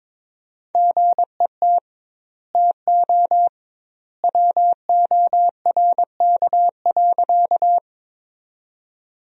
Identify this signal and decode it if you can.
Morse code, heard as a keyed tone, about 22 wpm, 700 Hz. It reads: GET TO WORK.
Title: 2x Word-Spacing - Sentences from Top 200 Words - 22wpm